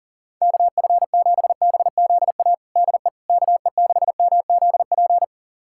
Morse code, heard as a keyed tone, about 37 wpm, 700 Hz. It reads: KF7BZA DE KE6MZP
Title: USA Callsign DE Callsign - 6 Characters - 40wpm